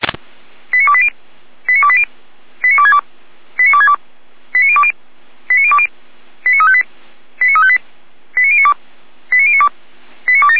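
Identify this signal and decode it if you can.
Five-tone selective call (ZVEI), heard as quick runs of five tones, tones of about 70 ms. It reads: 89280 89280 89262 89262 89020 89020 89358 89358 89092 89092 89270
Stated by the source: ZVEI five-tone selective call 89230 received by SDR.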